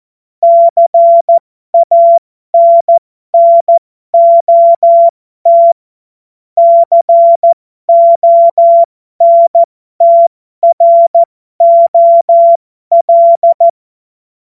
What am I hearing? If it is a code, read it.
Morse code, heard as a keyed tone, about 14 wpm, spaced as for 12 wpm, 680 Hz. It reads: CANNOT CONTROL
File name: Morsekurs / Witze